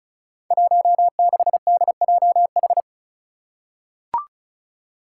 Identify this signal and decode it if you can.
Morse code, heard as a keyed tone, about 36 wpm, 700 Hz. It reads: 16DJH
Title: Five Challenging Letter-Number Combinations 35wpm